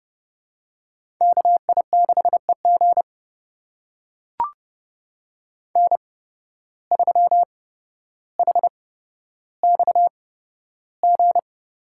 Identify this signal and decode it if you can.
Morse code, heard as a keyed tone, about 30 wpm, spaced as for 13 wpm, 700 Hz. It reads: KI6EG N3HXG
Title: USA Call Signs - 5 Characters - 8x Character Spacing - 30wpm